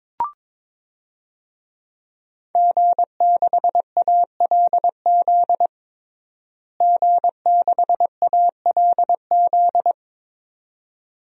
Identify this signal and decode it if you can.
Morse code, heard as a keyed tone, about 22 wpm, 700 Hz. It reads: G6ALZ G6ALZ
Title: DXCC-100 Call Signs - 5 Characters - Repeated 2x - 22wpm